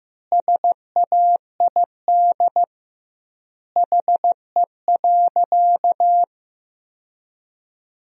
Morse code, heard as a keyed tone, about 15 wpm, 700 Hz. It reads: SAID HE.
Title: 2x Word-Spacing - Sentences from Top 100 Words - 15wpm